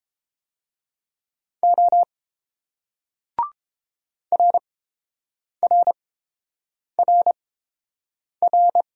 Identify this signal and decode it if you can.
Morse code, heard as a keyed tone, about 29 wpm, spaced as for 7 wpm, 700 Hz. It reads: O RRRR